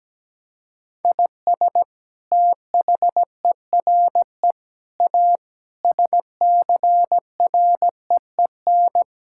Morse code, heard as a keyed tone, about 17 wpm, 700 Hz. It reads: IS THERE A SCREEN
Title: Sentences from Top 1000 Words - Review Part 6 - 17wpm